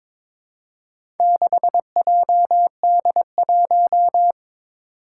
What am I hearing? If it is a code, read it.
Morse code, heard as a keyed tone, about 23 wpm, 700 Hz. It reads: 6JD1
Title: Four Challenging Letter-Number Combinations 22wpm